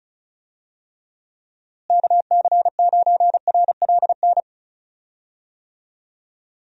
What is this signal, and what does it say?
Morse code, heard as a keyed tone, about 36 wpm, 700 Hz. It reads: KC9RLN
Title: USA Call Signs - 6 Characters - Repeated 2x - 35wpm